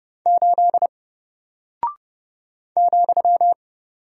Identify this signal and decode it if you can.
Morse code, heard as a keyed tone, about 30 wpm, 700 Hz. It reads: 8 ,
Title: Single Character (Letter:Number:Punctuation:Prosign) Alt 30wpm